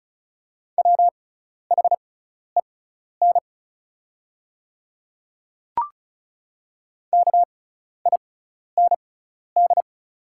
Morse code, heard as a keyed tone, about 34 wpm, spaced as for 12 wpm, 690 Hz. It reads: WHEN KIND
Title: 6x Character Spacing - Top 200 Words - 35wpm